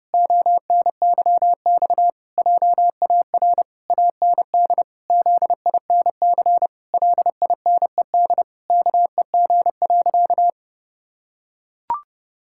Morse code, heard as a keyed tone, about 30 wpm, 700 Hz. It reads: ONYX JAR AND ZINC LINED KEG.
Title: Pangram Challenge v2 - 30wpm